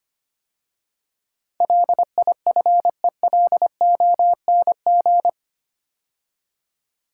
Morse code, heard as a keyed tone, about 25 wpm, 700 Hz. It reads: LIFELONG
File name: 8 Letter Words - First 3 Letters Given - 25wpm